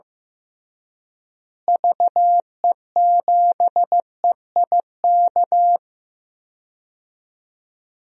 Morse code, heard as a keyed tone, about 16 wpm, 700 Hz. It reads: VE7EIK
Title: DXCC-100 Call Signs - 6 Characters - Repeated 3x - 15wpm